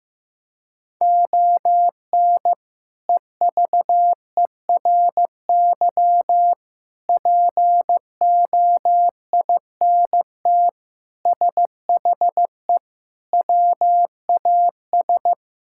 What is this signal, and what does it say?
Morse code, heard as a keyed tone, about 15 wpm, 700 Hz. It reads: ON EVERY POINT SHE WAS
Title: Sentences from Top 1000 Words - Review Part 6 - 15wpm